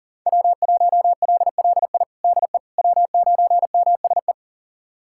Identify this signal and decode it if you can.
Morse code, heard as a keyed tone, about 37 wpm, 700 Hz. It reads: W1LLI DE W9MSE
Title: USA Callsign DE Callsign - 5 Characters - 40wpm